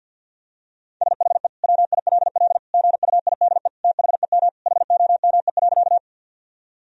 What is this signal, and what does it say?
Morse code, heard as a keyed tone, about 50 wpm, 700 Hz. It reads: SHE WILL GUIDE THEM HOME.
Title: Sentences from Top 1000 Words - Review Part 3 - 50wpm